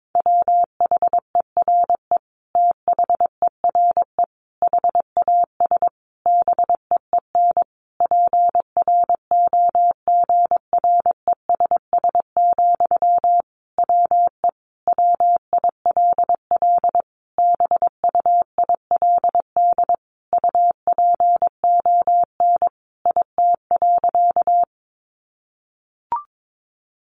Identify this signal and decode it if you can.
Morse code, heard as a keyed tone, about 22 wpm, 700 Hz. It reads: WHERE THERE HAS BEEN PROGRESS, WE WILL BUILD UPON IT.
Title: Sentences from Top 1400 Words 22wpm